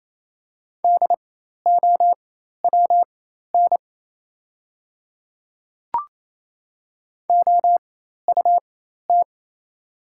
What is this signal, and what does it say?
Morse code, heard as a keyed tone, about 28 wpm, spaced as for 13 wpm, 690 Hz. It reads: DOWN OUT
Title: 4x Character Spacing - Top 100 Words - 28wpm